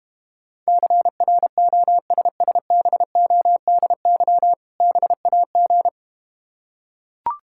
Morse code, heard as a keyed tone, about 32 wpm, 700 Hz. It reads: CROSSBODY BAG